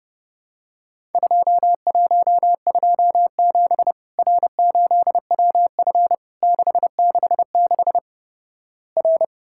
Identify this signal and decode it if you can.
Morse code, heard as a keyed tone, about 32 wpm, 700 Hz. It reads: 2127 R8WF 666 R
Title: Worked All Europe DX Contest - QTC Exchange - 30wpm